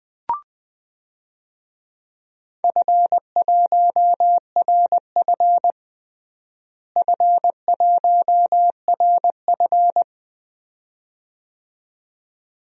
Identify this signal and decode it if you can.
Morse code, heard as a keyed tone, about 21 wpm, 700 Hz. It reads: F1RF F1RF
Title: DXCC-100 Call Signs - 4 Characters - Repeated 2x - 20wpm